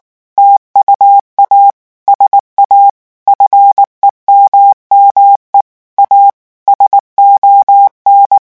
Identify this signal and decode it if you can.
Morse code, heard as a keyed tone, about 19 wpm, 800 Hz. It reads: TUA SA FEMME A SON
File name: morse mp3 / revision9 citations humoristiques vitesse 18 mots minute